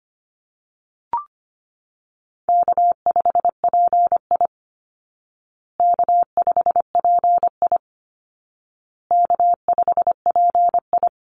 Morse code, heard as a keyed tone, about 25 wpm, 700 Hz. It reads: K5PI K5PI K5PI
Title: Easy Call Signs Repeated 3x - 25wpm